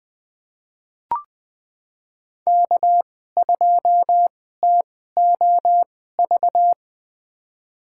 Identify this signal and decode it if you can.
Morse code, heard as a keyed tone, about 21 wpm, spaced as for 15 wpm, 700 Hz. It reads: K2TOV
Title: USA Call Signs - 5 Characters - 2x Character Spacing - 20wpm